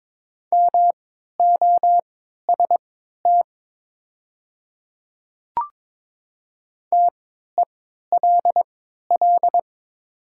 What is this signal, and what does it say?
Morse code, heard as a keyed tone, about 22 wpm, spaced as for 12 wpm, 700 Hz. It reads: MOST TELL
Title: 3x Character Spacing - Top 300 Words - 22wpm